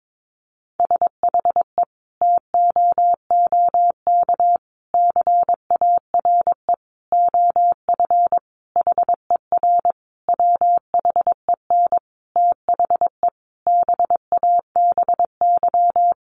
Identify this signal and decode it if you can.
Morse code, heard as a keyed tone, about 22 wpm, 700 Hz. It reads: SHE TOOK CARE OF HER WHEN THE BABY
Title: Sentences from Top 1300 Words 22wpm